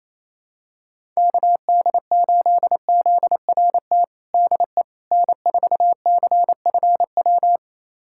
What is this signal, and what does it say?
Morse code, heard as a keyed tone, about 28 wpm, 700 Hz. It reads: KD8ZRT DE N4CFW
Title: USA Callsign DE Callsign 28wpm